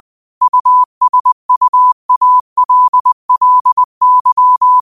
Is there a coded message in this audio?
Yes: Morse code from 0.4 to 4.8 s, heard as a keyed tone, about 20 wpm, 1000 Hz. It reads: USUALLY